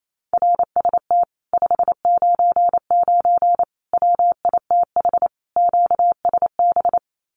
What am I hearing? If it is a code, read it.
Morse code, heard as a keyed tone, about 28 wpm, 700 Hz. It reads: RST 599 WITH QSB
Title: Common QSO Phrases Encore 28wpm